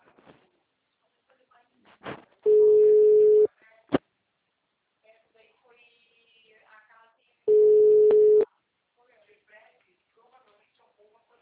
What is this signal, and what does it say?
Ringback tone (Europe, 425 Hz), with speech.
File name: out- -2003- - - .13064